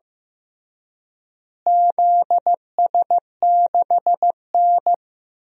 Morse code, heard as a keyed tone, about 15 wpm, 700 Hz. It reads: ZS6N
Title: DXCC-100 Call Signs - 4 Characters - Repeated 3x - 15wpm